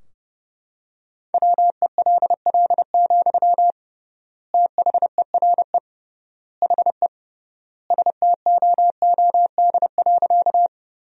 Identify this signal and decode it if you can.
Morse code, heard as a keyed tone, about 30 wpm, 700 Hz. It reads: WELL, THERE HE STOOD.
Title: 3x Word-Spacing - Sentences from Top 500 Words - 30wpm